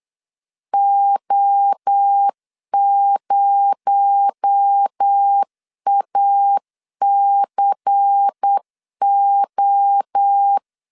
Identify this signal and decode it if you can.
Morse code, heard as a keyed tone, about 8 wpm, 800 Hz. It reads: O0ACO
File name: CW / DET / 1 6 1500C S B 8wpm 5142